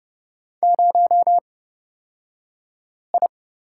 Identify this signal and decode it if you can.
Morse code, heard as a keyed tone, about 30 wpm, spaced as for 5 wpm, 700 Hz. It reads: 0I
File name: Lesson 37d - Letter x - Two Characters 30wpm